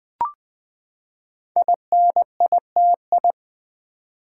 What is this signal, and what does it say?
Morse code, heard as a keyed tone, about 20 wpm, 700 Hz. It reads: INITI
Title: Build A Word - 5 Letters At A Time - 10-Letter Words - 20wpm